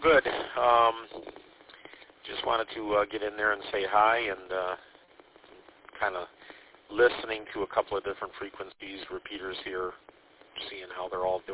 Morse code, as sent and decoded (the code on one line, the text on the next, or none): none